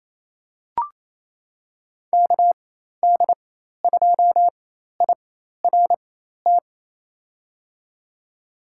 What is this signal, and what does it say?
Morse code, heard as a keyed tone, about 28 wpm, spaced as for 13 wpm, 700 Hz. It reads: KD2IRT
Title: USA Call Signs - 6 Characters - 4x Character Spacing - 28wpm